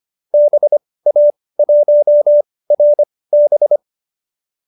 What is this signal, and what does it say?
Morse code, heard as a keyed tone, about 25 wpm, spaced as for 18 wpm, 590 Hz. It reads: BA1RB